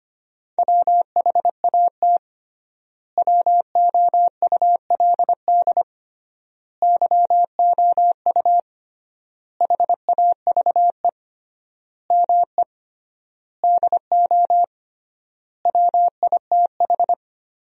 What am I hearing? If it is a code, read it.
Morse code, heard as a keyed tone, about 25 wpm, 700 Hz. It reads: WHAT WOULD YOU HAVE ME DO WITH
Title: 3x Word-Spacing - Sentences from Top 300 Words - 25wpm